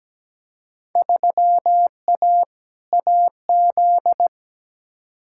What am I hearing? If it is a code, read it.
Morse code, heard as a keyed tone, about 17 wpm, 700 Hz. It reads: 3A AZ
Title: Field Day Exchanges - 17wpm